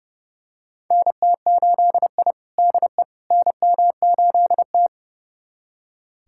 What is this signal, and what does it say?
Morse code, heard as a keyed tone, about 30 wpm, 700 Hz. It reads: NT8I DE NM8T
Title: USA Callsign DE Callsign - 4 Characters - 30wpm